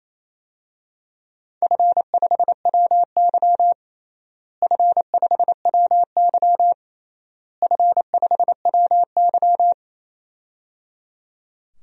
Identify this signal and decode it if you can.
Morse code, heard as a keyed tone, about 27 wpm, 700 Hz. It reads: F5WY F5WY F5WY
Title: DXCC-100 Call Signs - 4 Characters - Repeated 3x - 28wpm